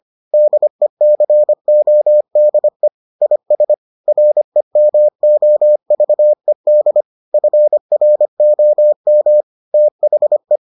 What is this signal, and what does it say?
Morse code, heard as a keyed tone, about 25 wpm, 600 Hz. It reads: DECODE IS REMOVED FROM THE